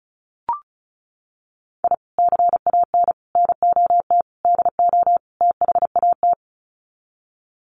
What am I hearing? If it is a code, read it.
Morse code, heard as a keyed tone, about 34 wpm, 700 Hz. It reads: I CAN NOT DO THAT
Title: Sets of 5 Words Encore2 35wpm